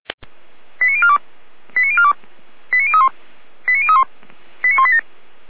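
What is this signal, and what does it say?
Five-tone selective call (ZVEI), heard as quick runs of five tones, tones of about 70 ms. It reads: 89042 89042 89031 89031 89187
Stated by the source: ZVEI five-tone call received by SDR.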